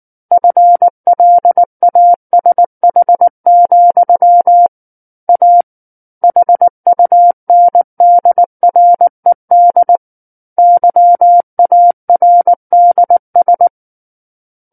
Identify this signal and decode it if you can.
Morse code, heard as a keyed tone, about 19 wpm, 700 Hz. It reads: FLASH, A HUNDRED YARDS
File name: Book One / 10-Chapter 10 - 19 WPM